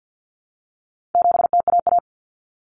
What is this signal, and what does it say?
Morse code, heard as a keyed tone, about 50 wpm, 700 Hz. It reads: 7TAA
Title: Common Cut Numbers - 4-Digit 50wpm